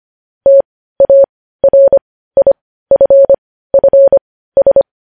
Morse code, heard as a keyed tone, about 25 wpm, spaced as for 15 wpm, 550 Hz. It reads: TARIFFS
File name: podcast / News headlines-1x-25x15